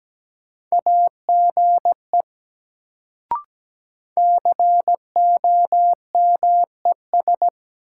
Morse code, heard as a keyed tone, about 17 wpm, 700 Hz. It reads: AGE COMES
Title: Sets of 3 Words - New Word Drill - 17wpm